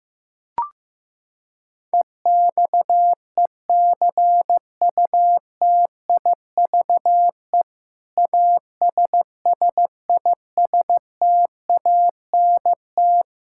Morse code, heard as a keyed tone, about 15 wpm, 700 Hz. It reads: EXECUTIVE ASSISTANT
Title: Practice QSO - Exchange 3 - More Info - 15wpm